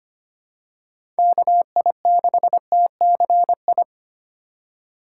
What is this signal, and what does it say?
Morse code, heard as a keyed tone, about 25 wpm, 690 Hz. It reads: KI6TCI
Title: USA Call Signs - 6 Characters - 3x Character Spacing - 25wpm